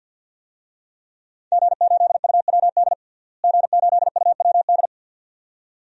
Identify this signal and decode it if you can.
Morse code, heard as a keyed tone, about 52 wpm, 700 Hz. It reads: G8UWD G8UWD
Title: DXCC-100 Call Signs - 5 Characters - Repeated 2x - 50wpm